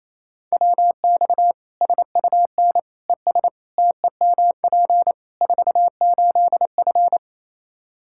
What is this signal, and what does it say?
Morse code, heard as a keyed tone, about 27 wpm, 700 Hz. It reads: WX SUN ES TEMP 48F
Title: K1USN SST QSO Practice 28wpm